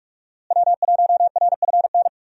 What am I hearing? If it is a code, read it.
Morse code, heard as a keyed tone, about 44 wpm, 700 Hz. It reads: W1RFN
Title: USA Call Signs - 5 Characters - Repeated 3x - 45wpm